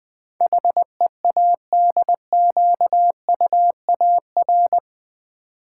Morse code, heard as a keyed tone, about 20 wpm, 700 Hz. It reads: HEADQUAR
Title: Build A Word - 4 Letters At A Time - 12-Letter Words - 20wpm